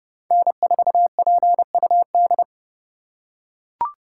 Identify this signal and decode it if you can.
Morse code, heard as a keyed tone, about 30 wpm, 700 Hz. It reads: N4PUD
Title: USA Call Signs - 5 Characters - 2x Character Spacing - 30wpm